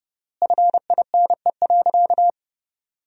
Morse code, heard as a keyed tone, about 30 wpm, 700 Hz. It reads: FINE.
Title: 3x Word-Spacing - Sentences from Top 500 Words - 30wpm